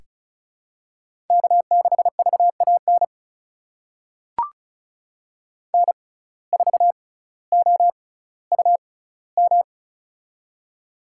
Morse code, heard as a keyed tone, about 36 wpm, spaced as for 18 wpm, 700 Hz. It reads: K6VAN N4OUM
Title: USA Call Signs - 5 Characters - 6x Character Spacing - 35wpm